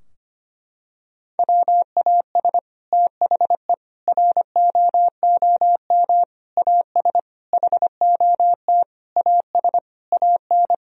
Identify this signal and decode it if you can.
Morse code, heard as a keyed tone, about 25 wpm, 700 Hz. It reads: WAS THE ROOM AS HOT AS AN
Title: Sentences from Top 1000 Words - Review Part 4 - 25wpm